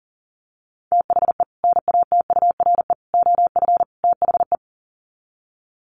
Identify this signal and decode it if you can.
Morse code, heard as a keyed tone, about 40 wpm, 700 Hz. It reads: THE NATURE OF THE
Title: Sets of 4 Words 40wpm